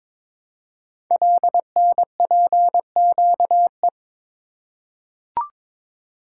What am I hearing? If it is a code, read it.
Morse code, heard as a keyed tone, about 22 wpm, 700 Hz. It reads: LNPQE
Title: Code Groups - 5 Letters - 22wpm